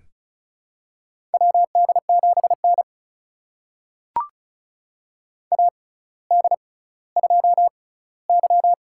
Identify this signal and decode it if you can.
Morse code, heard as a keyed tone, about 36 wpm, spaced as for 18 wpm, 700 Hz. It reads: WD7N AD2Y
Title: USA Call Signs - 4 Characters - 6x Character Spacing - 35wpm